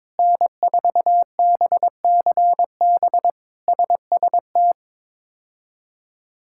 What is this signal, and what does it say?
Morse code, heard as a keyed tone, about 22 wpm, 700 Hz. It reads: N4BCB SST
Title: K1USN SST QSO Practice - 22wpm